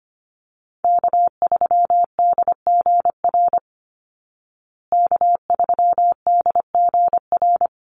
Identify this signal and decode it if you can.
Morse code, heard as a keyed tone, about 25 wpm, 700 Hz. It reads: K3DGR K3DGR
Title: Call Signs Repeated 2x - 25wpm